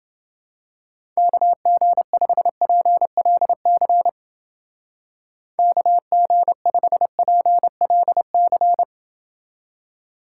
Morse code, heard as a keyed tone, about 29 wpm, 700 Hz. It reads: KG5PLC KG5PLC